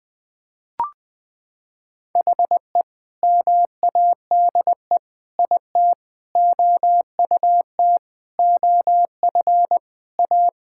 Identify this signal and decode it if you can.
Morse code, heard as a keyed tone, about 20 wpm, 700 Hz. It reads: HE MADE IT OUT OF A
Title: Sentences with 3-Word Phrases - 20wpm